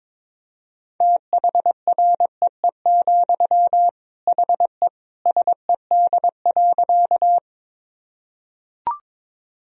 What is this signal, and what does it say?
Morse code, heard as a keyed tone, about 21 wpm, 700 Hz. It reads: THREE, HE SED.
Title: Sentences from Top 200 Words and Abbreviations 22wpm